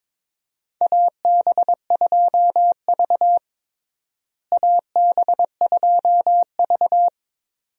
Morse code, heard as a keyed tone, about 22 wpm, 700 Hz. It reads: AB2V AB2V
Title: USA Call Signs - 4 Characters - Repeated 2x - 22wpm